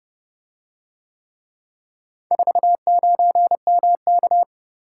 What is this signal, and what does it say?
Morse code, heard as a keyed tone, about 30 wpm, 700 Hz. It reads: 49MK